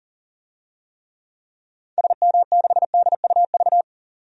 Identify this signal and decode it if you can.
Morse code, heard as a keyed tone, about 37 wpm, 700 Hz. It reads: SM6DUV
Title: DXCC-100 Call Signs - 6 Characters - 40wpm